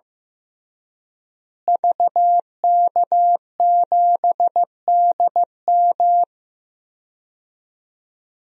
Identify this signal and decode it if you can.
Morse code, heard as a keyed tone, about 15 wpm, 700 Hz. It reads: VK7DM
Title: DXCC-100 Call Signs - 5 Characters - Repeated 2x - 15wpm